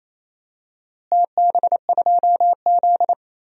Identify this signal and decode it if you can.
Morse code, heard as a keyed tone, about 28 wpm, 700 Hz. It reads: TB2Z
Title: DXCC-100 Call Signs - 4 Characters - Repeated 2x - 28wpm